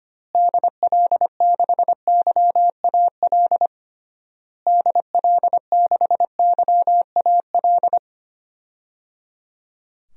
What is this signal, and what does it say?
Morse code, heard as a keyed tone, about 25 wpm, 700 Hz. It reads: DL6YAL DL6YAL
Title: DXCC-100 Call Signs - 6 Characters - Repeated 3x - 25wpm